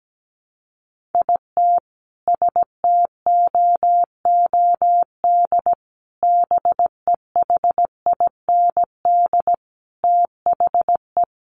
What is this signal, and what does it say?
Morse code, heard as a keyed tone, about 17 wpm, 700 Hz. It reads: IT STOOD BEHIND THE
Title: Sentences from Top 500 Words 17wpm